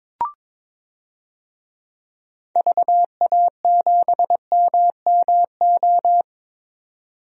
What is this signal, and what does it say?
Morse code, heard as a keyed tone, about 22 wpm, 700 Hz. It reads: VA7MMO
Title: DXCC-100 Call Signs - 6 Characters - 22wpm